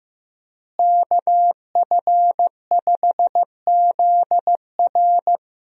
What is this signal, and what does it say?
Morse code, heard as a keyed tone, about 15 wpm, 700 Hz. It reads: KF5ZR
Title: USA Call Signs - 5 Characters - Repeated 2x - 15wpm